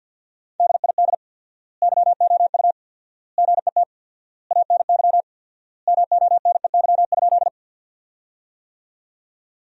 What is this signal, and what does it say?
Morse code, heard as a keyed tone, about 53 wpm, 700 Hz. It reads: DID YOU GET ANY MONEY?
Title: 4x Word-Spacing - Sentences from Top 400 Words - 50wpm